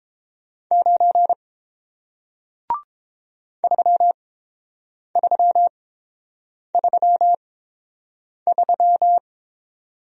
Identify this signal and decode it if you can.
Morse code, heard as a keyed tone, about 29 wpm, spaced as for 7 wpm, 700 Hz. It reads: 9 3333